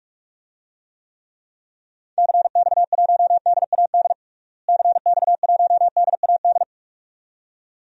Morse code, heard as a keyed tone, about 44 wpm, 700 Hz. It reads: CX1DAD CX1DAD
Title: DXCC-100 Call Signs - 6 Characters - Repeated 2x - 45wpm